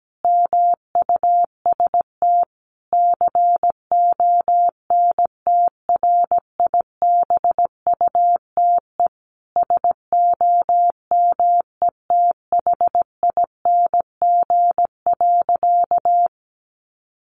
Morse code, heard as a keyed tone, about 17 wpm, 700 Hz. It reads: MUST CONTRIBUTE SOMETHING.
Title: Sentences from Top 1400 Words 17wpm